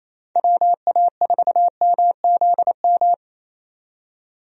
Morse code, heard as a keyed tone, about 27 wpm, 700 Hz. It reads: WA4MZM
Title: USA Call Signs - 6 Characters - 28wpm